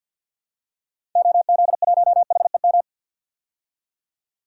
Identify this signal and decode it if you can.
Morse code, heard as a keyed tone, about 52 wpm, 700 Hz. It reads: OZ1HEM
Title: DXCC-100 Call Signs - 6 Characters - 50wpm